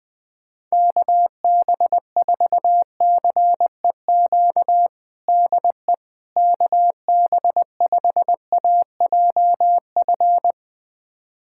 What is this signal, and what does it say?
Morse code, heard as a keyed tone, about 21 wpm, 700 Hz. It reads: KB4CEQ DE KB5AJF
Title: USA Callsign DE Callsign 20wpm